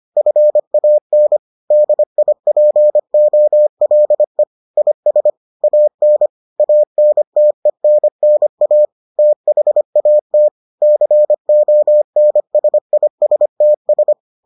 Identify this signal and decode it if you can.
Morse code, heard as a keyed tone, about 25 wpm, 600 Hz. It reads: FAN DIPOLE IS AN ANTENNA THAT CONSISTS